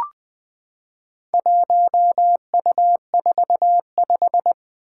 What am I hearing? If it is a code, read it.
Morse code, heard as a keyed tone, about 20 wpm, 700 Hz. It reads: 1U45
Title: Four Challenging Letter-Number Combinations 20wpm